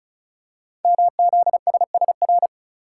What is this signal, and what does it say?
Morse code, heard as a keyed tone, about 36 wpm, 700 Hz. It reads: MZSSR
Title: Code Groups - 5 Letters - 35wpm